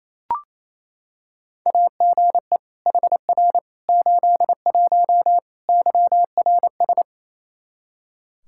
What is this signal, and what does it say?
Morse code, heard as a keyed tone, about 28 wpm, 700 Hz. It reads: AGE HR 81 YRS
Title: Practice QSO - Exchange 3 - More Info - 28wpm